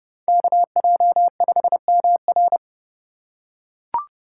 Morse code, heard as a keyed tone, about 32 wpm, 700 Hz. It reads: KJ5MR